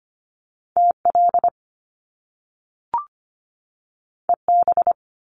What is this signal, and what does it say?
Morse code, heard as a keyed tone, about 25 wpm, 700 Hz. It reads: TL EB
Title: Top Two Letter Combinations in Words 25wpm